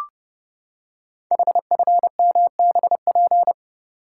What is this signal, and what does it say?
Morse code, heard as a keyed tone, about 30 wpm, 700 Hz. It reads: HFMBP
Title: Code Groups - 5 Letter:Numbers - 30wpm